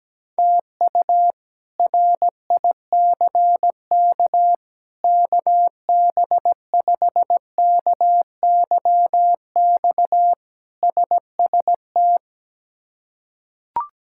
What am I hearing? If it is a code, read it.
Morse code, heard as a keyed tone, about 17 wpm, 700 Hz. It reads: TU RICK KB5KYX SST
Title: K1USN SST QSO Practice - 17wpm